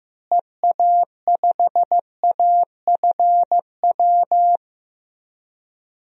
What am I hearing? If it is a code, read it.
Morse code, heard as a keyed tone, about 15 wpm, 700 Hz. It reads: EA5AFW
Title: DXCC-100 Call Signs - 6 Characters - Repeated 2x - 15wpm